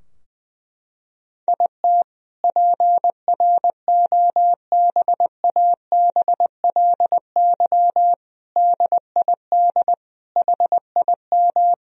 Morse code, heard as a keyed tone, about 21 wpm, 700 Hz. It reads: IT PROBABLY DID HIM
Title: Sentences from Top 1000 Words - Review Part 4 - 20wpm